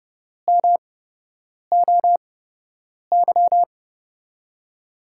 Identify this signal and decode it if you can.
Morse code, heard as a keyed tone, about 30 wpm, spaced as for 8 wpm, 700 Hz. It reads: MOY
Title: USA Call Signs - 6 Characters - 8x Character Spacing - 30wpm